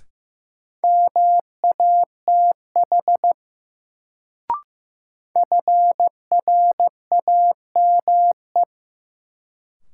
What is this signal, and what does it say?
Morse code, heard as a keyed tone, about 15 wpm, 700 Hz. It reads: MATH FRAME
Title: Sentences from Top 1000 Words - Review Part 6 - New Word Drill - 15wpm